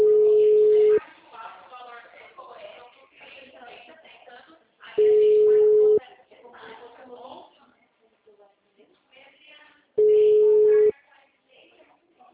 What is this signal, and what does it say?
Ringback tone (Europe, 425 Hz), with speech.